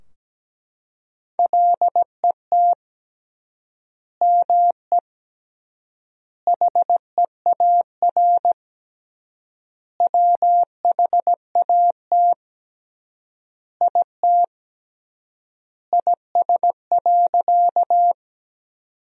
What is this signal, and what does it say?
Morse code, heard as a keyed tone, about 17 wpm, 700 Hz. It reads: LET ME HEAR WHAT IT IS.
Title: 3x Word-Spacing - Sentences from Top 300 Words - 17wpm